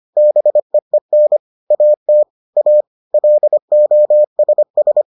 Morse code, heard as a keyed tone, about 25 wpm, 600 Hz. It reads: BEEN AT A LOSS